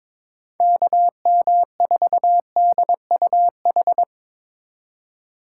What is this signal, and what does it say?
Morse code, heard as a keyed tone, about 24 wpm, 700 Hz. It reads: KM4DUH
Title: Speed Racing - USA 2x3 Call Signs 22wpm